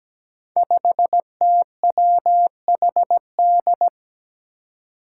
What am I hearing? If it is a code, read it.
Morse code, heard as a keyed tone, about 17 wpm, 700 Hz. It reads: 5TWHD
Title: Five Challenging Letter-Number Combinations 17wpm